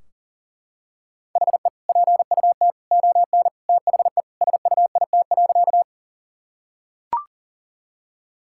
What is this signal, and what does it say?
Morse code, heard as a keyed tone, about 40 wpm, 700 Hz. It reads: HE PUT ON THE SUIT.
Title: Sentences from Top 1600 Words 40wpm